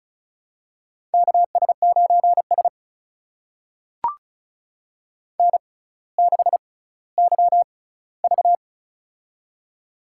Morse code, heard as a keyed tone, about 36 wpm, spaced as for 18 wpm, 700 Hz. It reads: KS9S N6YV
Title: USA Call Signs - 4 Characters - 6x Character Spacing - 35wpm